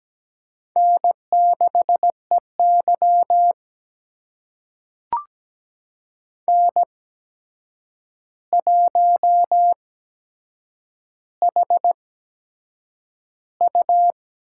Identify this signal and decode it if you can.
Morse code, heard as a keyed tone, about 17 wpm, spaced as for 7 wpm, 700 Hz. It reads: N6EY N1HU
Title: USA Call Signs - 4 Characters - 8x Character Spacing - 17wpm